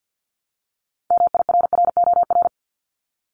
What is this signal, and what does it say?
Morse code, heard as a keyed tone, about 50 wpm, 700 Hz. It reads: MIRROR
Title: Multi-letter Phonemes Repeated 3x 50wpm